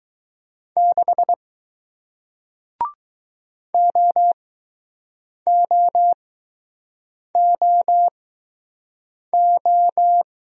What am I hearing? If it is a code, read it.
Morse code, heard as a keyed tone, about 20 wpm, spaced as for 6 wpm, 700 Hz. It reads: 6 OOOO